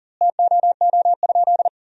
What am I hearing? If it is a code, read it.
Morse code, heard as a keyed tone, about 40 wpm, 700 Hz. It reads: TOO?
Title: 2x Word-Spacing - Sentences from Top 200 Words - 40wpm